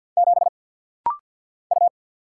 Morse code, heard as a keyed tone, about 50 wpm, 700 Hz. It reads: 7 U
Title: Single Letter-Number - Rapid-Fire 50wpm